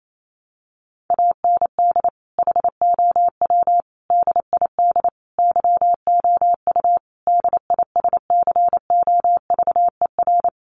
Morse code, heard as a keyed tone, about 28 wpm, 700 Hz. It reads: AND HOW DID YOU DISCOVER
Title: Sentences from Top 1100 Words 28wpm